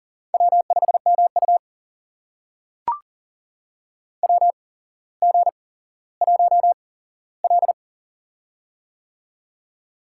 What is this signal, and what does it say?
Morse code, heard as a keyed tone, about 40 wpm, spaced as for 17 wpm, 700 Hz. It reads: W5MU WG1L
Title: USA Call Signs - 4 Characters - 8x Character Spacing - 40wpm